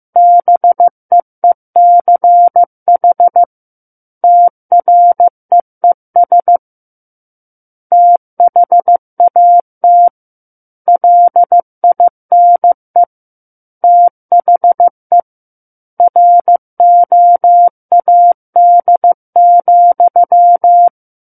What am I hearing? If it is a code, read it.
Morse code, heard as a keyed tone, about 15 wpm, 700 Hz. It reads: BEECH TREES THAT LINE THE ROAD,
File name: Chapter 6 - 15 WPM